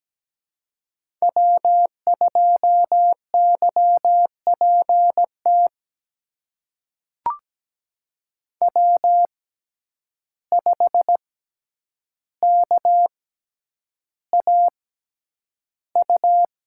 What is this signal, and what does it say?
Morse code, heard as a keyed tone, about 17 wpm, spaced as for 9 wpm, 700 Hz. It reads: W2YPT W5KAU
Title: USA Call Signs - 5 Characters - 6x Character Spacing - 17wpm